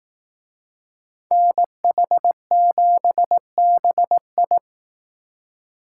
Morse code, heard as a keyed tone, about 18 wpm, 700 Hz. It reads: NH7BI
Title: Speed Racing - USA 2x2 Call Signs 15wpm